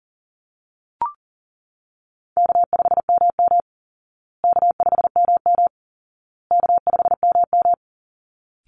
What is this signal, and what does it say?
Morse code, heard as a keyed tone, about 40 wpm, 700 Hz. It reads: K5MM K5MM K5MM
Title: Easy Call Signs Repeated 3x - 40wpm